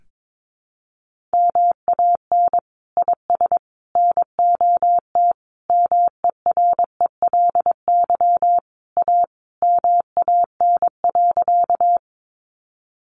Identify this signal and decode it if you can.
Morse code, heard as a keyed tone, about 22 wpm, 700 Hz. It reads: MAN IS NOT MERELY A MAN.
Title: Sentences from Top 1400 Words 22wpm